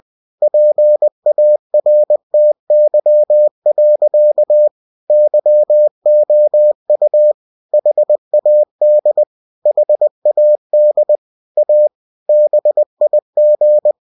Morse code, heard as a keyed tone, about 20 wpm, 600 Hz. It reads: PARTY. YOU HAD HAD A BIG
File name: cw-books / winnie-the-pooh / winnie-20wpm-page-0005